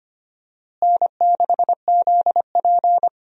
Morse code, heard as a keyed tone, about 25 wpm, 700 Hz. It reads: N6ZP
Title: USA Call Signs - 4 Characters - Repeated 3x - 25wpm